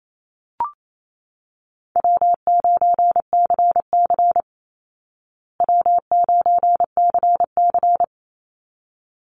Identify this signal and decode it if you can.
Morse code, heard as a keyed tone, about 28 wpm, 700 Hz. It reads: W9CC W9CC
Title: Easy Call Signs Repeated 2x - 28wpm